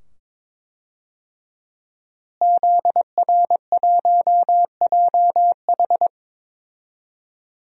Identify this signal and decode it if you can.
Morse code, heard as a keyed tone, about 22 wpm, 700 Hz. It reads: ZR1JH
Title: DXCC-100 Call Signs - 5 Characters - Repeated 2x - 22wpm